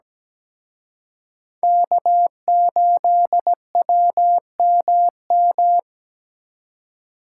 Morse code, heard as a keyed tone, about 20 wpm, 700 Hz. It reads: K8WMM
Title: Speed Racing - USA 1x3 Call Signs 17wpm